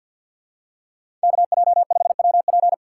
Morse code, heard as a keyed tone, about 51 wpm, 700 Hz. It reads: KJ5WP
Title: USA Call Signs - 5 Characters - 3x Character Spacing - 50wpm